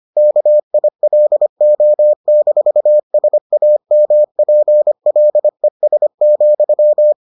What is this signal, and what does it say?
Morse code, heard as a keyed tone, about 25 wpm, 600 Hz. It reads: KILO-SAMPLES,